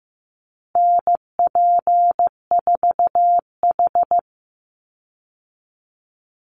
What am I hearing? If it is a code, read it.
Morse code, heard as a keyed tone, about 15 wpm, 700 Hz. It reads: NP4H
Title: Call Signs Repeated 3x - 15wpm